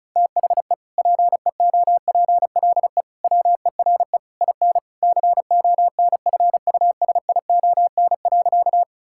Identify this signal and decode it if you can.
Morse code, heard as a keyed tone, about 36 wpm, 700 Hz. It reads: THE PEOPLE WERE IN CONFUSION.
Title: Sentences from Top 1600 Words 35wpm